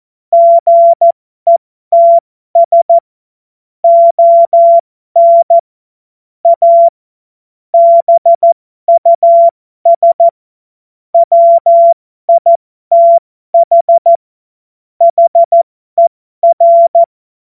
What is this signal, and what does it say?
Morse code, heard as a keyed tone, about 14 wpm, spaced as for 12 wpm, 680 Hz. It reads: GETS ON A BUS WITH HER
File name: Morsekurs / Witze